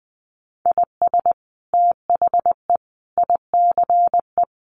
Morse code, heard as a keyed tone, about 19 wpm, 700 Hz. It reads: IS THE ICE
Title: Sentences from Top 1500 Words 20wpm